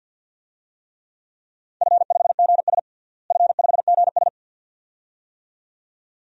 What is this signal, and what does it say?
Morse code, heard as a keyed tone, about 52 wpm, 700 Hz. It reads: F5GS F5GS
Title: DXCC-100 Call Signs - 4 Characters - Repeated 2x - 50wpm